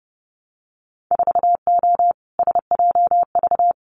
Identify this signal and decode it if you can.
Morse code, heard as a keyed tone, about 30 wpm, 700 Hz. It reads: 4O SJV
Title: Winter Field Day Exchanges Encore 30wpm